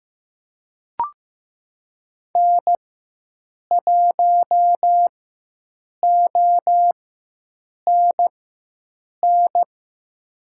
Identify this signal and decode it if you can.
Morse code, heard as a keyed tone, about 15 wpm, spaced as for 7 wpm, 700 Hz. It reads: N1ONN
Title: USA Call Signs - 5 Characters - 4x Character Spacing - 15wpm